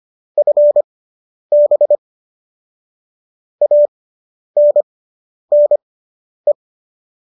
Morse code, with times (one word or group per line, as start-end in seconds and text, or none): 0.4-2.0 FB
3.6-6.5 ANNE